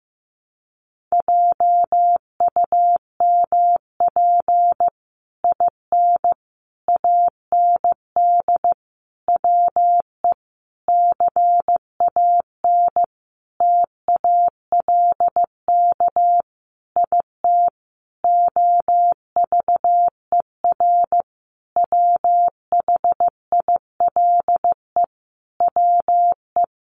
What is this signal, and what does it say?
Morse code, heard as a keyed tone, about 15 wpm, 700 Hz. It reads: JUMP IN AND WE CAN TALK IT OVER WHILE WE
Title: Sentences from Top 1300 Words 15wpm